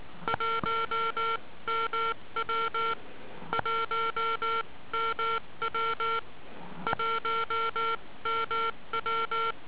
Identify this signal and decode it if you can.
Morse code, heard as a keyed tone, about 19 wpm, spaced as for 16 wpm, 1360 Hz. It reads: 1MW 1MW 1MW